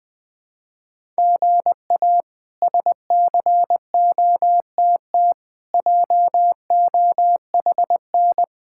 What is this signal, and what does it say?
Morse code, heard as a keyed tone, about 20 wpm, 700 Hz. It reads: GA SCOTT JOHN
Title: K1USN SST QSO Practice - 20wpm